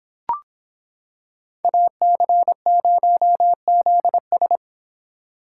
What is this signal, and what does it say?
Morse code, heard as a keyed tone, about 25 wpm, 700 Hz. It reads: AC0ZS